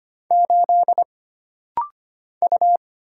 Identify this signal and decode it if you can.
Morse code, heard as a keyed tone, about 25 wpm, 700 Hz. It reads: 8 U
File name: Single Character (Letter:Number:Punctuation:Prosign) - Rapid-Fire - 25wpm